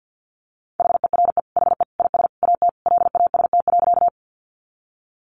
Morse code, heard as a keyed tone, about 48 wpm, 700 Hz. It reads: HERE HE IS AT LAST.
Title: Sentences from Top 300 Words 50wpm